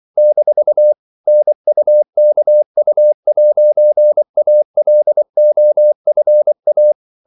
Morse code, heard as a keyed tone, about 24 wpm, 600 Hz. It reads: - NUKU'ALOFA
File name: CW / Practice / NationalCapitals24